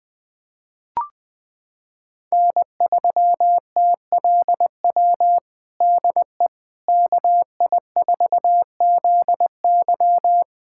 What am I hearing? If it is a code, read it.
Morse code, heard as a keyed tone, about 20 wpm, 700 Hz. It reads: N3TLW DE KI4ZY
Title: USA Callsign DE Callsign - 5 Characters - 20wpm